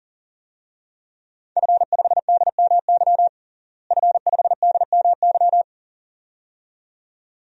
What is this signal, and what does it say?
Morse code, heard as a keyed tone, about 40 wpm, 700 Hz. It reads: F5DMY F5DMY
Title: DXCC-100 Call Signs - 5 Characters - Repeated 2x - 40wpm